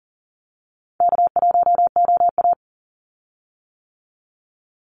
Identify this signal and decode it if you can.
Morse code, heard as a keyed tone, about 40 wpm, 700 Hz. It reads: K1OA
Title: Easy Call Signs - 40wpm